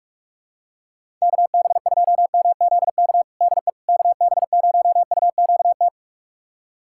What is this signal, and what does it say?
Morse code, heard as a keyed tone, about 46 wpm, 700 Hz. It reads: KB2MZK DE KB0UQT
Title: USA Callsign DE Callsign 45wpm